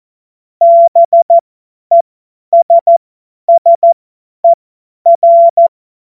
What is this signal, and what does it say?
Morse code, heard as a keyed tone, about 14 wpm, spaced as for 10 wpm, 680 Hz. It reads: BESSER
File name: Morsekurs / Witze / Witz 12. April 2022 50 F70